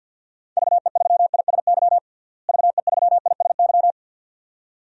Morse code, heard as a keyed tone, about 49 wpm, 700 Hz. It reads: VE3ISY VE3ISY
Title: DXCC-100 Call Signs - 6 Characters - Repeated 2x - 50wpm